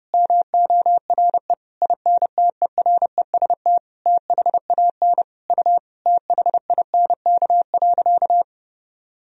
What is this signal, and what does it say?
Morse code, heard as a keyed tone, about 30 wpm, 700 Hz. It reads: MORE INTEREST THAN U THINK.
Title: Sentences from Top 400 Words and Abbreviations 30wpm